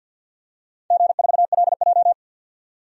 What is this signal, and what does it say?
Morse code, heard as a keyed tone, about 53 wpm, 700 Hz. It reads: G4LJ
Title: DXCC-100 Call Signs - 4 Characters - 50wpm